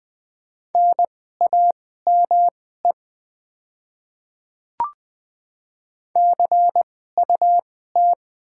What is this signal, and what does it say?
Morse code, heard as a keyed tone, about 20 wpm, spaced as for 14 wpm, 700 Hz. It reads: NAME CUT
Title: 2x Character Spacing - Top 300 Words - 20wpm